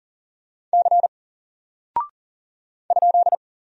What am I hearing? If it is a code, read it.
Morse code, heard as a keyed tone, about 40 wpm, 700 Hz. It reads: C ?
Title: Single Character (Letter:Number:Punctuation:Prosign) 40wpm